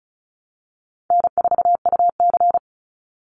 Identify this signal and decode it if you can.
Morse code, heard as a keyed tone, about 34 wpm, 700 Hz. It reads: N4UC
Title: Call Signs - 35wpm